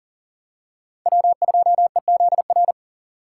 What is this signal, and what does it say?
Morse code, heard as a keyed tone, about 40 wpm, 700 Hz. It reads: W2EZR
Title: USA Call Signs - 5 Characters - 6x Character Spacing - 40wpm